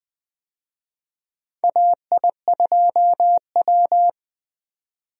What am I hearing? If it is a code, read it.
Morse code, heard as a keyed tone, about 20 wpm, 700 Hz. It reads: AI2W
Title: USA Call Signs - 4 Characters - 20wpm